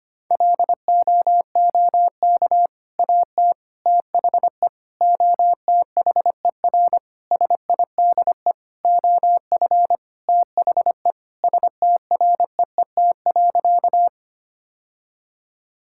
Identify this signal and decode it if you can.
Morse code, heard as a keyed tone, about 25 wpm, 700 Hz. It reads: LOOK AT THE OTHER SIDE OF THE STREET.
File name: practice / 25wpm / Sentences with 3-Word Phrases - 25wpm